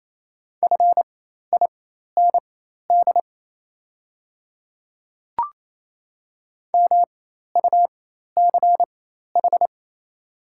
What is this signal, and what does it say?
Morse code, heard as a keyed tone, about 27 wpm, spaced as for 13 wpm, 700 Hz. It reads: FIND MUCH
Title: 4x Character Spacing - Top 200 Words - 28wpm